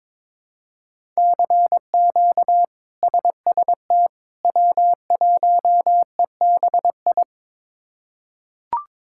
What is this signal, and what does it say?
Morse code, heard as a keyed tone, about 22 wpm, 700 Hz. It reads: CQ SST W1EBI
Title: K1USN SST QSO Practice - 22wpm